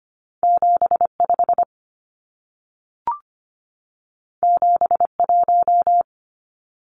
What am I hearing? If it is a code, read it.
Morse code, heard as a keyed tone, about 25 wpm, 700 Hz. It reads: 75 71
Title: Numbers - 2-Digit 25wpm